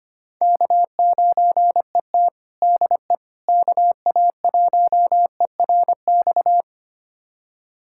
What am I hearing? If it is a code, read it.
Morse code, heard as a keyed tone, about 25 wpm, 700 Hz. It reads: K9ET DE KA1ERX
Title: USA Callsign DE Callsign 25wpm